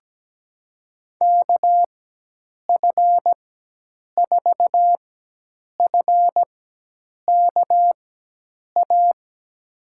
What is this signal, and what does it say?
Morse code, heard as a keyed tone, about 17 wpm, spaced as for 8 wpm, 700 Hz. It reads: KF4FKA
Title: USA Call Signs - 6 Characters - 4x Character Spacing - 17wpm